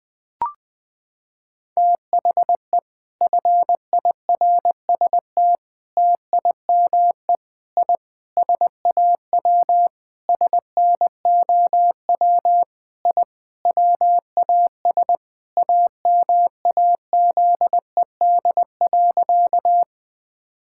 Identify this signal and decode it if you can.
Morse code, heard as a keyed tone, about 20 wpm, 700 Hz. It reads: THE FIRST TIME I SAW SNOW I WAS AMAZED.
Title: Sentences with 4-Word Phrases - 20wpm